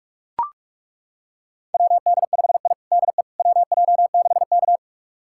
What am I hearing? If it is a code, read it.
Morse code, heard as a keyed tone, about 42 wpm, 700 Hz. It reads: WD5I DE WJ6K
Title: USA Callsign DE Callsign - 4 Characters - 45wpm